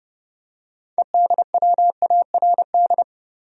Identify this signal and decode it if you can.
Morse code, heard as a keyed tone, about 32 wpm, 700 Hz. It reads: EDWARD
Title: USA Common Names - 30wpm